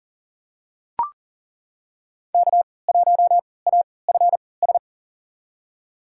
Morse code, heard as a keyed tone, about 40 wpm, spaced as for 23 wpm, 700 Hz. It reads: K1AFS
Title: USA Call Signs - 5 Characters - 3x Character Spacing - 40wpm